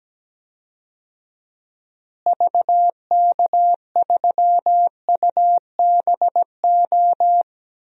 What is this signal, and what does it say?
Morse code, heard as a keyed tone, about 17 wpm, 700 Hz. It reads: VK3UBO
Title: DXCC-100 Call Signs - 6 Characters - Repeated 2x - 17wpm